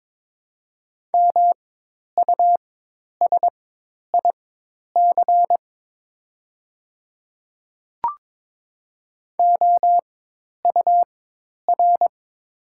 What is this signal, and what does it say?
Morse code, heard as a keyed tone, about 22 wpm, spaced as for 10 wpm, 700 Hz. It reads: MUSIC OUR
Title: 4x Character Spacing - Top 300 Words - 22wpm